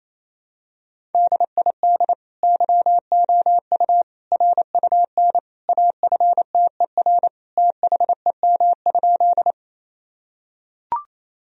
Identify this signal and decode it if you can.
Morse code, heard as a keyed tone, about 28 wpm, 700 Hz. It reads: DID YOU RUN AFTER THEM?
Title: Sentences from Top 1000 Words - Review Part 1 - 28wpm